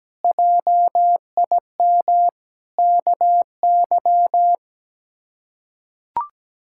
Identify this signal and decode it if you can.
Morse code, heard as a keyed tone, about 17 wpm, 700 Hz. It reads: JIM KY
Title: K1USN SST QSO Practice - 17wpm